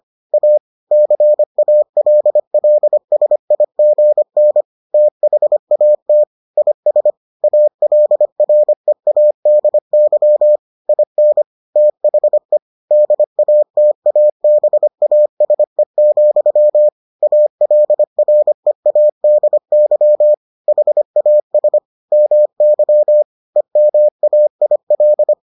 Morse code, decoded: A CALLSIGN THAT IS ALREADY IN THE DATABASE, ALREADY HAS MY EMAIL

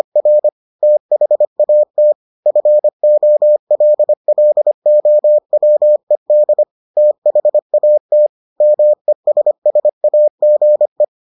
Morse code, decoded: R THAT FOLLOWED THAT MESSAGE

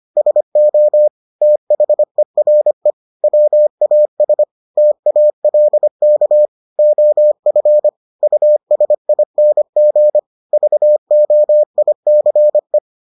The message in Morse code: SO THERE WAS TALK OF USING VOICE

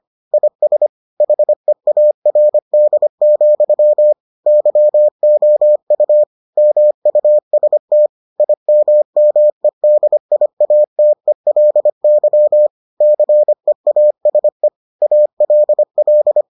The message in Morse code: IS HEARD, YOU MUST IMMEDIATELY CEASE ALL

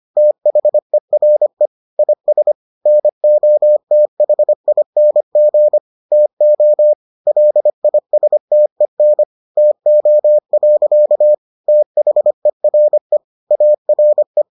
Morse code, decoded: THERE IS NOTHING TO LISTEN TO. THERE ARE

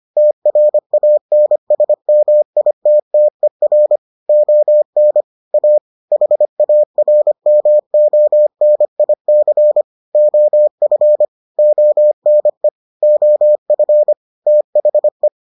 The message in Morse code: TRANSMITTER ON A HARMONIC OF ONE OF THE

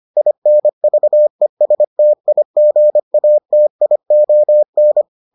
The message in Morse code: INVESTIGATION

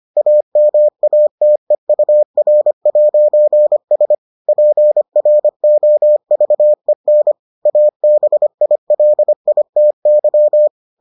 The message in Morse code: AMATEUR'S PROVEN ABILITY

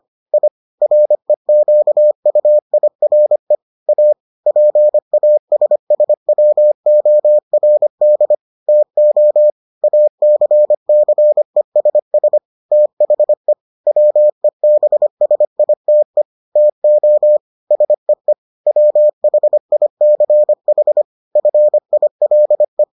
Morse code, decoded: I REQUIRE A PASSWORD TO ACCESS THE WEBSITE TO SEE WHICH FILE

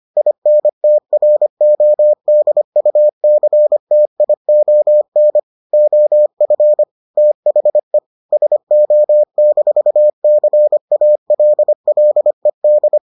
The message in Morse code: INTRODUCTION OF THE SO-CALLED

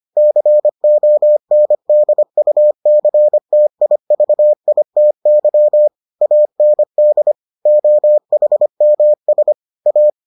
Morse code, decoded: CONDUCTIVITY AND OHMS A

